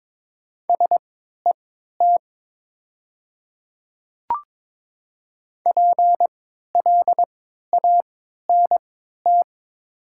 Morse code, heard as a keyed tone, about 22 wpm, spaced as for 13 wpm, 700 Hz. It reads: SET PLANT
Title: 3x Character Spacing - Top 500 Words - 22wpm